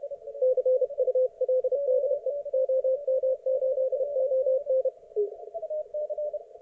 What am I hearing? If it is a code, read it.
Morse code, heard as a keyed tone, about 31 wpm, 520 Hz. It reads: CUL DE OM8ON